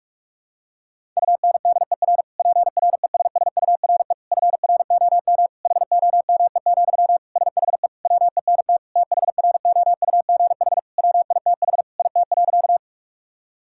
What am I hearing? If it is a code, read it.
Morse code, heard as a keyed tone, about 46 wpm, 700 Hz. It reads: UNDER PRESSURE FROM HOME, SHE WENT THROUGH WITH IT.
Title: Sentences from Top 1000 Words - Review Part 6 - 45wpm